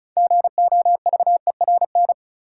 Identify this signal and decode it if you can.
Morse code, heard as a keyed tone, about 36 wpm, 700 Hz. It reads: GOVERN
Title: Sentences from Top 1000 Words - Review Part 2 - New Word Drill - 35wpm